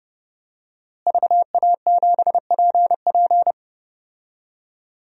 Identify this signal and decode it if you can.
Morse code, heard as a keyed tone, about 31 wpm, 700 Hz. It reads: VA7PP
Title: DXCC-100 Call Signs - 5 Characters - Repeated 3x - 30wpm